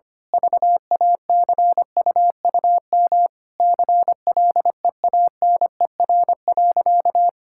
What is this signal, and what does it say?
Morse code, heard as a keyed tone, about 25 wpm, 700 Hz. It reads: VACUUM CLEANER.
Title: Pangram Challenge v2 - 25wpm